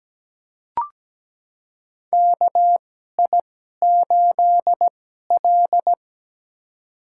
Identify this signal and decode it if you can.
Morse code, heard as a keyed tone, about 17 wpm, spaced as for 12 wpm, 700 Hz. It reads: KI8L
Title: USA Call Signs - 4 Characters - 2x Character Spacing - 17wpm